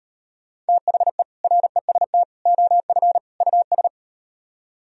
Morse code, heard as a keyed tone, about 38 wpm, 700 Hz. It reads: THE REST OF US